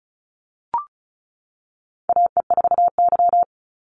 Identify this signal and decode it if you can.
Morse code, heard as a keyed tone, about 34 wpm, 700 Hz. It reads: AE4Y
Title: NA SOTA Chaser Callsigns 35wpm